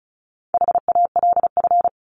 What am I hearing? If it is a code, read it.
Morse code, heard as a keyed tone, about 34 wpm, 700 Hz. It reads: HALF
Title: Top 500 Words Repeated 35wpm